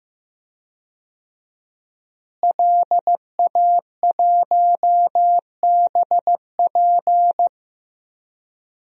Morse code, heard as a keyed tone, about 15 wpm, 700 Hz. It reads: LA1BP
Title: DXCC-100 Call Signs - 5 Characters - Repeated 3x - 15wpm